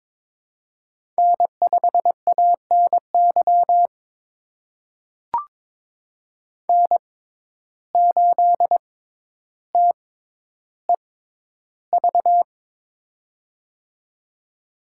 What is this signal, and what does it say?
Morse code, heard as a keyed tone, about 22 wpm, spaced as for 11 wpm, 700 Hz. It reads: N5ANY N8TEV
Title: USA Call Signs - 6x Character Spacing - 22wpm